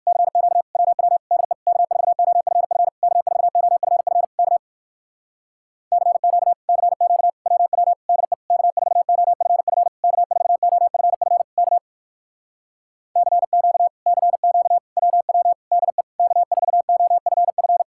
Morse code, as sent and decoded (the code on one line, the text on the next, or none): -.-. --.- / .-- .-- / -.. / -.- ....- --- ..-. ..-. / -.- ....- --- ..-. ..-. / -.- / -.-. --.- / -.-. --.- / .-- .-- / -.. / -.- ....- --- ..-. ..-. / -.- ....- --- ..-. ..-. / -.- / -.-. --.- / -.-. --.- / .-- .-- / -.. / -.- ....- --- ..-. ..-.
CQ WW DE K4OFF K4OFF K CQ CQ WW DE K4OFF K4OFF K CQ CQ WW DE K4OFF